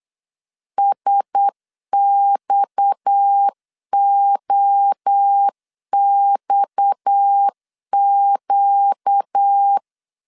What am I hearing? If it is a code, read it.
Morse code, heard as a keyed tone, about 8 wpm, 800 Hz. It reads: SXOXQ
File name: LEZIONI / 6 PKX0 / 6 PKX 8wpm 3041